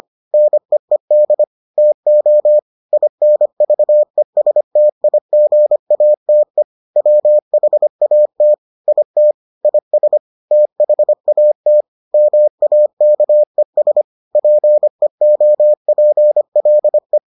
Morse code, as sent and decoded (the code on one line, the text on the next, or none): -. . -.. / - --- / .. -. ...- . ... - .. --. .- - . / .-- .... .- - / .. - / .. ... / - .... .- - / -- .- -.- . ... / .--. --- .--. .-.. .
NEED TO INVESTIGATE WHAT IT IS THAT MAKES PEOPLE